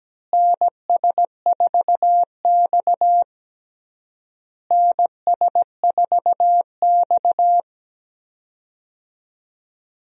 Morse code, heard as a keyed tone, about 17 wpm, 700 Hz. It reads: NS4X NS4X
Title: USA Call Signs - 4 Characters - Repeated 3x - 17wpm